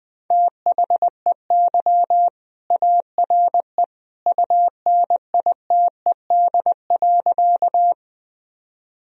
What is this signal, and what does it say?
Morse code, heard as a keyed tone, about 21 wpm, 700 Hz. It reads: THEY ARE UNITED.
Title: Sentences from Top 1000 Words - Review Part 4 - 20wpm